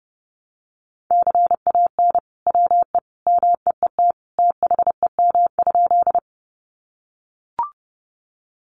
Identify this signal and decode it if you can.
Morse code, heard as a keyed tone, about 30 wpm, 700 Hz. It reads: CAN WE MEET THEM?
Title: Sentences from Top 1100 Words 30wpm